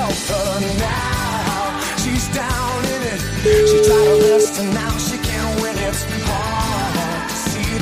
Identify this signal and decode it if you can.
Ringback tone (Europe, 425 Hz), with noise.